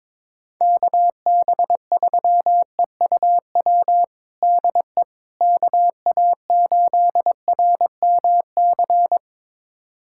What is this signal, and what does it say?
Morse code, heard as a keyed tone, about 22 wpm, 700 Hz. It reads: KB3EUW DE KA8RMC
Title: USA Callsign DE Callsign - 6 Characters - 22wpm